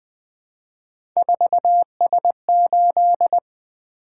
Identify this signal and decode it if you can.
Morse code, heard as a keyed tone, about 21 wpm, 700 Hz. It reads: 4S8
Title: Three Letter-Number Combinations - 20wpm